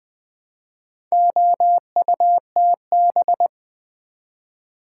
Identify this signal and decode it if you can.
Morse code, heard as a keyed tone, about 20 wpm, 700 Hz. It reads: OUTB
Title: Build A Word - 2 Letters At A Time - 8-Letter Words - 20wpm